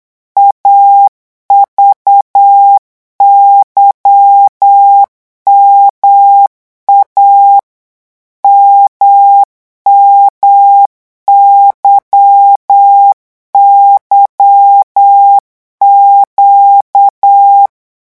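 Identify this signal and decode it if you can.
Morse code, heard as a keyed tone, about 8 wpm, 800 Hz. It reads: AVYMA MMYYQ
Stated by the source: Morse code revision lesson at 8 words per minute, from a French amateur radio course.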